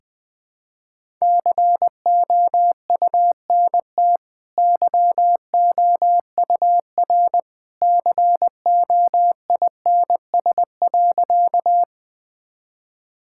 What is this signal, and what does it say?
Morse code, heard as a keyed tone, about 20 wpm, 700 Hz. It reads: COUNT YOUR COINS.
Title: Easy Sentences with Hints - 20wpm